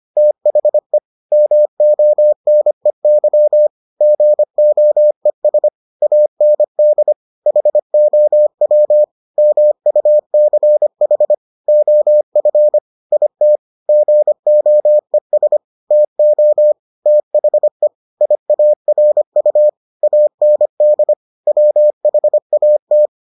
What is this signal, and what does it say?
Morse code, heard as a keyed tone, about 25 wpm, 600 Hz. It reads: THE MONEY GOES AND HOW MUCH OF IT GOES TO THE IARU AND WHAT